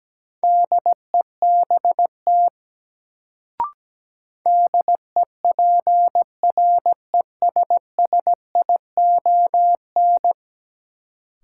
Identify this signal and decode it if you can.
Morse code, heard as a keyed tone, about 17 wpm, 700 Hz. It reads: DEBT DEPRESSION
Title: Sentences from Top 1300 Words - New Word Drill Alt - 17wpm